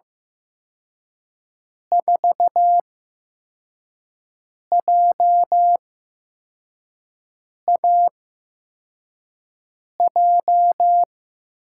Morse code, heard as a keyed tone, about 15 wpm, spaced as for 4 wpm, 700 Hz. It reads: 4JAJ
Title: USA Call Signs - 6 Characters - 8x Character Spacing - 15wpm